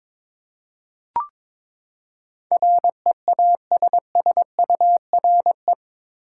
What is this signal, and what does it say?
Morse code, heard as a keyed tone, about 22 wpm, 700 Hz. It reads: REASSURE
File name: practice / 22wpm / Eight Letter Words - 22wpm